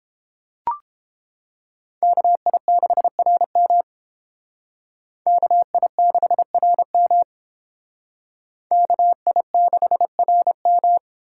Morse code, heard as a keyed tone, about 30 wpm, 700 Hz. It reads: KI6RM KI6RM KI6RM